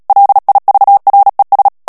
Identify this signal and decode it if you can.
Morse code, heard as a keyed tone, about 37 wpm, 800 Hz. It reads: LIVRES
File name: CW / FICHIERS-MP3 / FRANCAIS TEXTE CLAIR 2 VITESSE 35 MOTS MINUTES